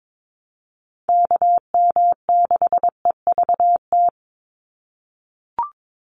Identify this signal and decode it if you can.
Morse code, heard as a keyed tone, about 22 wpm, 700 Hz. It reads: KM6EVT
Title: USA Call Signs 22wpm